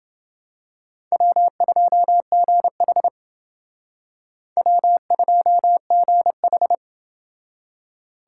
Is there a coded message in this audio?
Yes: Morse code from 1.1 to 6.7 s, heard as a keyed tone, about 28 wpm, 700 Hz. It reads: W2GH W2GH